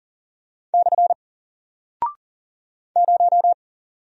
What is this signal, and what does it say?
Morse code, heard as a keyed tone, about 40 wpm, 700 Hz. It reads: / 0
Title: Single Character (Letter:Number:Punctuation:Prosign) Alt 40wpm